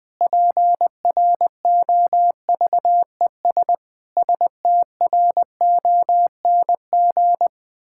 Morse code, heard as a keyed tone, about 20 wpm, 700 Hz. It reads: PROVES STRONG
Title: Pangram Challenge v2 - 20wpm